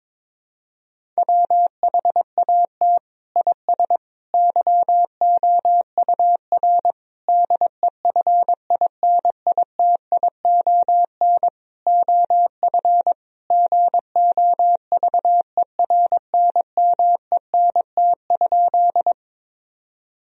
Morse code, heard as a keyed tone, about 22 wpm, 700 Hz. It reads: WHAT IS YOUR DEFINITION OF GOVERNMENT?
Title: Sentences from Top 1000 Words - Review Part 4 - 22wpm